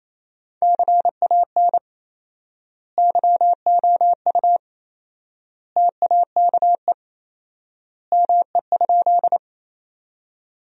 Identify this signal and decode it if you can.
Morse code, heard as a keyed tone, about 28 wpm, 700 Hz. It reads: CAN YOU TAKE ME?
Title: 4x Word-Spacing - Sentences from Top 200 Words - 28wpm